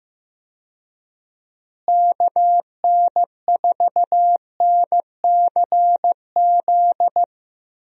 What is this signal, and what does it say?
Morse code, heard as a keyed tone, about 15 wpm, 700 Hz. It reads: KN4NCZ
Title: USA Call Signs - 6 Characters - Repeated 3x - 15wpm